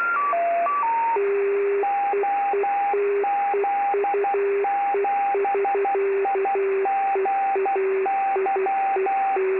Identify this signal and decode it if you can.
Morse code, heard as a keyed tone, about 12 wpm, 790 Hz, with radio noise. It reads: OZ7IGY
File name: extra / websdr recording .3kHz OZ7IGY